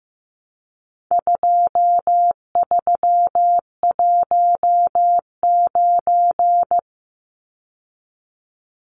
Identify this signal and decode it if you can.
Morse code, heard as a keyed tone, about 15 wpm, 700 Hz. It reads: 2319
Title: Numbers - 4-Digit 15wpm